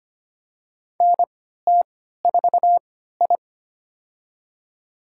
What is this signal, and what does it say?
Morse code, heard as a keyed tone, about 25 wpm, spaced as for 14 wpm, 700 Hz. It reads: NT4I
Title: USA Call Signs - 3x Character Spacing - 25wpm